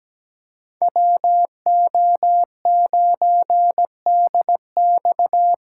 Morse code, heard as a keyed tone, about 17 wpm, 700 Hz. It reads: WO9DX
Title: USA Call Signs - 3x Character Spacing - 17wpm